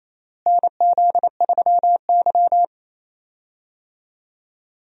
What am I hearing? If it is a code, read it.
Morse code, heard as a keyed tone, about 28 wpm, 700 Hz. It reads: NZ3Y
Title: USA Call Signs - 4 Characters - Repeated 2x - 28wpm